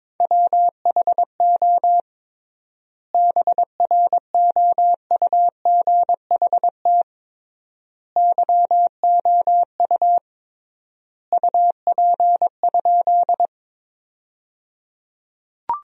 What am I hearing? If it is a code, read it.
Morse code, heard as a keyed tone, about 21 wpm, 700 Hz. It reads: WHO BROUGHT YOU UP?
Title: 3x Word-Spacing - Sentences from Top 500 Words - 22wpm